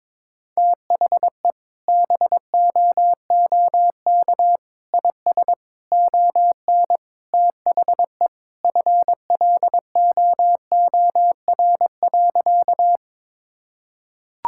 Morse code, heard as a keyed tone, about 21 wpm, 700 Hz. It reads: THE BOOK IS ON THE FLOOR.
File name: practice / 22wpm / Sentences with 3-Word Phrases - 22wpm